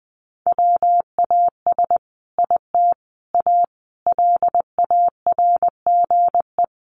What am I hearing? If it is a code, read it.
Morse code, heard as a keyed tone, about 20 wpm, 700 Hz. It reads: WAS IT A LARGE
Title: Sentences from Top 1300 Words 20wpm